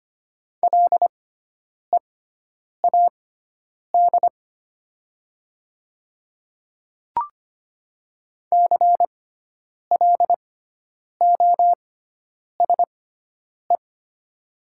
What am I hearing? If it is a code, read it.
Morse code, heard as a keyed tone, about 25 wpm, spaced as for 9 wpm, 700 Hz. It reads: LEAD CLOSE
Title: 6x Character Spacing - Top 500 Words - 25wpm